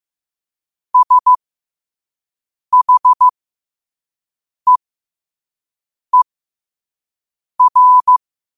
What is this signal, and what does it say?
Morse code, heard as a keyed tone, about 15 wpm, spaced as for 5 wpm, 1000 Hz. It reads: SHEER
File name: qotdfiles / 5wpm